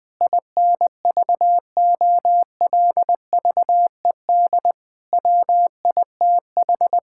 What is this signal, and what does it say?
Morse code, heard as a keyed tone, about 20 wpm, 700 Hz. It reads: INVOLVED WITH
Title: Sentences from Top 1000 Words - Review Part 5 - 20wpm